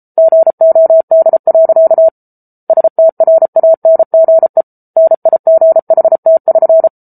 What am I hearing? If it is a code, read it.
Morse code, heard as a keyed tone, about 34 wpm, 650 Hz. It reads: GOD. STRANGE NIGHT<SN>